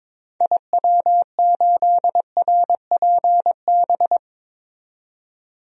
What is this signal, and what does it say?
Morse code, heard as a keyed tone, about 22 wpm, 700 Hz. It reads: IW8RPB
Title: DXCC-100 Call Signs - 6 Characters - 22wpm